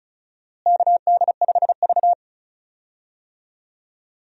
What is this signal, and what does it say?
Morse code, heard as a keyed tone, about 36 wpm, 700 Hz. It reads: KD5V
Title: USA Call Signs - 4 Characters - Repeated 3x - 35wpm